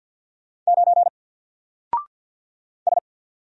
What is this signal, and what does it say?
Morse code, heard as a keyed tone, about 50 wpm, 700 Hz. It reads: 9 S
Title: Single Letter-Number Alt 50wpm